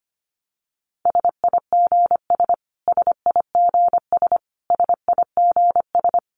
Morse code, heard as a keyed tone, about 25 wpm, 700 Hz. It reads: SIGS SIGS SIGS
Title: Abbreviations Repeated 3x - 25wpm